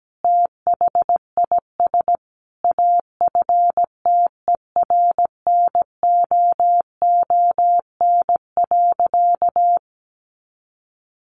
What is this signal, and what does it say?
Morse code, heard as a keyed tone, about 17 wpm, 700 Hz. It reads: THIS AFTERNOON.
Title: Sentences from Top 1500 Words 17wpm